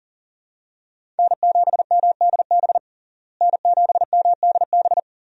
Morse code, heard as a keyed tone, about 37 wpm, 700 Hz. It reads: N7MDB N7MDB
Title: USA Call Signs - 5 Characters - Repeated 3x - 40wpm